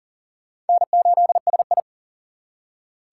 Morse code, heard as a keyed tone, about 51 wpm, 700 Hz. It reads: N8SI